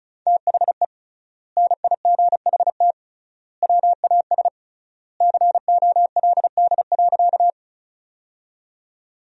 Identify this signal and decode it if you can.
Morse code, heard as a keyed tone, about 34 wpm, 700 Hz. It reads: THE NIGHT WAS COLD.
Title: 3x Word-Spacing - Sentences from Top 400 Words - 35wpm